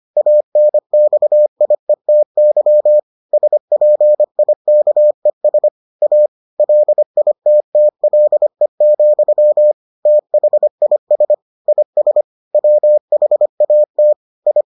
Morse code, heard as a keyed tone, about 25 wpm, 600 Hz. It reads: ANXIETY SPIKES A LITTLE, THIS IS WHAT I